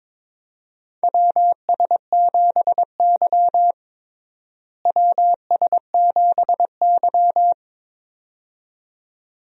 Morse code, heard as a keyed tone, about 21 wpm, 700 Hz. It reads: WS7Y WS7Y
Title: USA Call Signs - 4 Characters - Repeated 2x - 22wpm